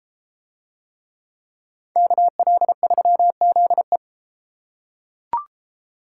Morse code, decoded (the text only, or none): KL3ZE